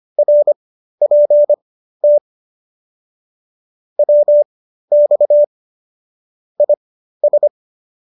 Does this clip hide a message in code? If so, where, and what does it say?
Yes: Morse code from 0.2 to 7.5 s, heard as a keyed tone, about 25 wpm, spaced as for 13 wpm, 590 Hz. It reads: RPT WX IS